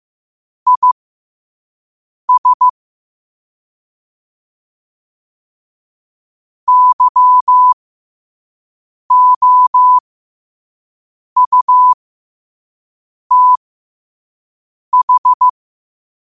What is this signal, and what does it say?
Morse code, heard as a keyed tone, about 15 wpm, spaced as for 5 wpm, 1000 Hz. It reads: IS YOUTH
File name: qotdfiles / 5wpm